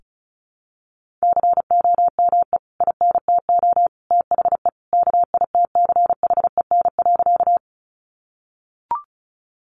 Morse code, heard as a keyed tone, about 34 wpm, 700 Hz. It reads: COME INTO THE KITCHEN.
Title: Sentences from Top 1400 Words 35wpm